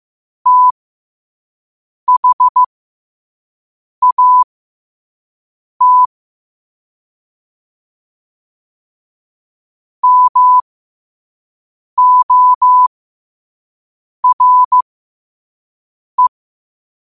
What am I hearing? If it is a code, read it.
Morse code, heard as a keyed tone, about 15 wpm, spaced as for 5 wpm, 1000 Hz. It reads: THAT MORE
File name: qotdfiles / 5wpm